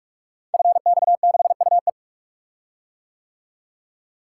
Morse code, heard as a keyed tone, about 44 wpm, 700 Hz. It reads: FX6UE
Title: Code Groups - 5 Letter:Numbers - 45wpm